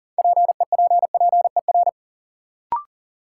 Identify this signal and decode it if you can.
Morse code, heard as a keyed tone, about 40 wpm, 700 Hz. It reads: PEPPER
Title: Sets of 5 Words - New Word Drill - 40wpm